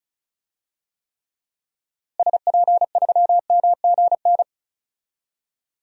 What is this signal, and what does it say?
Morse code, heard as a keyed tone, about 36 wpm, 700 Hz. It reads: SP3MGN
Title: DXCC-100 Call Signs - 6 Characters - Repeated 3x - 35wpm